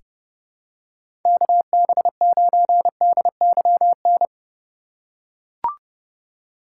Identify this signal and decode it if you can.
Morse code, heard as a keyed tone, about 30 wpm, 700 Hz. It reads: KB9DYN
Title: USA Call Signs - 6 Characters - 2x Character Spacing - 30wpm